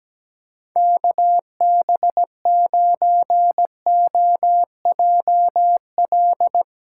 Morse code, heard as a keyed tone, about 17 wpm, 700 Hz. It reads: KB9OJL
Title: USA Callsign DE Callsign 17wpm